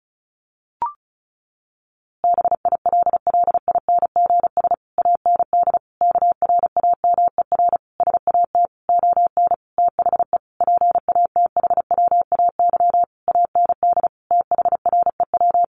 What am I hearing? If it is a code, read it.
Morse code, heard as a keyed tone, about 34 wpm, 700 Hz. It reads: BILLINGS AND KRAMER SAT ON THE PATHWAY AND THREW
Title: A Sound Of Thunder 35wpm